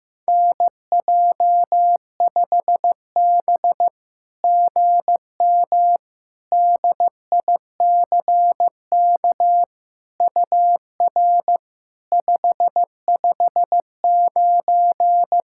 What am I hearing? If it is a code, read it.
Morse code, heard as a keyed tone, about 16 wpm, 700 Hz. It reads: NJ5B GM DICK UR 559
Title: SOTA QSO Practice 15wpm